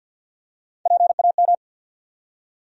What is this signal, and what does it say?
Morse code, heard as a keyed tone, about 52 wpm, 700 Hz. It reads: PAM
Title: USA Common Names - Max 3 Letters - 50wpm